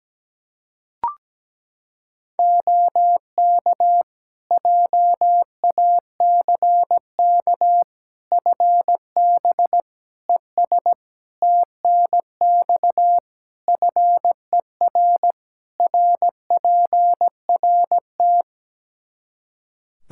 Morse code, heard as a keyed tone, about 17 wpm, 700 Hz. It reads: OK JACK FB ES TNX FER RPRT
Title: Practice QSO - Exchange 2 - Info - 17wpm